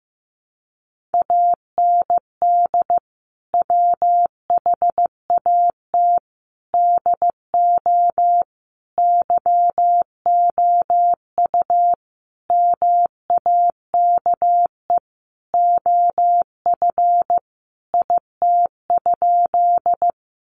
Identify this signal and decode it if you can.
Morse code, heard as a keyed tone, about 15 wpm, 700 Hz. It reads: AND WHAT DO YOU MAKE OF IT?
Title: Sentences from Top 100 Words 15wpm